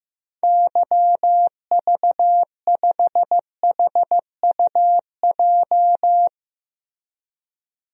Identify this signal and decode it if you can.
Morse code, heard as a keyed tone, about 15 wpm, 700 Hz. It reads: YV5HUJ
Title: DXCC-100 Call Signs - 6 Characters - Repeated 2x - 15wpm